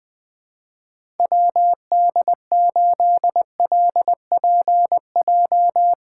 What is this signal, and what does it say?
Morse code, heard as a keyed tone, about 20 wpm, 700 Hz. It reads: WD8LPJ
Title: USA Call Signs - 6 Characters - 3x Character Spacing - 20wpm